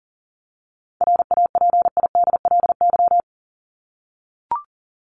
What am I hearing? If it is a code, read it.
Morse code, heard as a keyed tone, about 38 wpm, 700 Hz. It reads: RAPIDLY
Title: LY Suffix - Top 100 Words with Base Words 40wpm